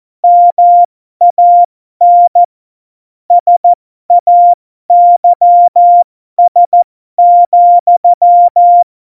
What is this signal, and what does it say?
Morse code, heard as a keyed tone, about 14 wpm, spaced as for 12 wpm, 700 Hz. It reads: MAN SAYS,
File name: Morsekurs / Witze